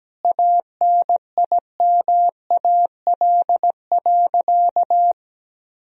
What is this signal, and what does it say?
Morse code, heard as a keyed tone, about 17 wpm, 700 Hz. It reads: ANIMAL.
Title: Sentences from Top 1000 Words - Review Part 1 - 17wpm